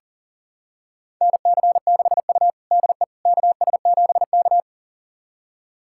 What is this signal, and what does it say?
Morse code, heard as a keyed tone, about 40 wpm, 700 Hz. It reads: NC6U DE KS7K
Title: USA Callsign DE Callsign - 4 Characters - 40wpm